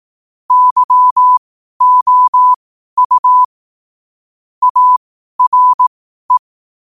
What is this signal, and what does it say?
Morse code, heard as a keyed tone, about 18 wpm, spaced as for 12 wpm, 1000 Hz. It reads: YOU ARE
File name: qotdfiles / 12wpm